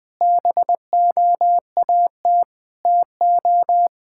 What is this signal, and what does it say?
Morse code, heard as a keyed tone, about 21 wpm, 700 Hz. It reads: BOAT TO
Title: Sentences from Top 1000 Words - Review Part 5 - 20wpm